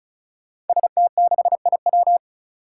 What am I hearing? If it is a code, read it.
Morse code, heard as a keyed tone, about 34 wpm, 700 Hz. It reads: ST6IW
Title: Five Challenging Letter-Number Combinations 35wpm